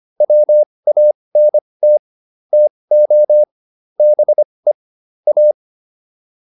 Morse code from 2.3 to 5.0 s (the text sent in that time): TO BE